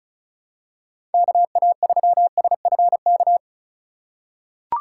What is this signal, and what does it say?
Morse code, heard as a keyed tone, about 36 wpm, 700 Hz. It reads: KA3SFK
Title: USA Call Signs - 6 Characters - 3x Character Spacing - 35wpm